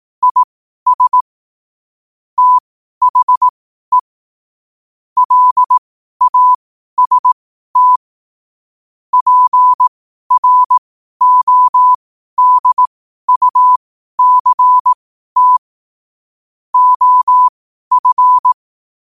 Morse code, heard as a keyed tone, about 18 wpm, spaced as for 12 wpm, 1000 Hz. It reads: IS THE LAST PRODUCT OF
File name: qotdfiles / 12wpm